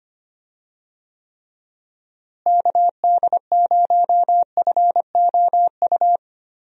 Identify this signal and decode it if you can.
Morse code, heard as a keyed tone, about 25 wpm, 700 Hz. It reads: KD0FOU
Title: USA Call Signs - 6 Characters - Repeated 2x - 25wpm